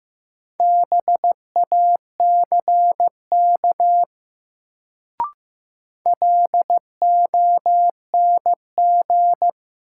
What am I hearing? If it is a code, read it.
Morse code, heard as a keyed tone, about 15 wpm, 700 Hz. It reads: BACK LONG
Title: Sets of 2 Words - New Word Drill - 15wpm